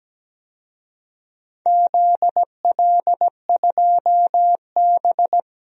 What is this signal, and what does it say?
Morse code, heard as a keyed tone, about 17 wpm, 700 Hz. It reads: ZL2B
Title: DXCC-100 Call Signs - 4 Characters - Repeated 2x - 17wpm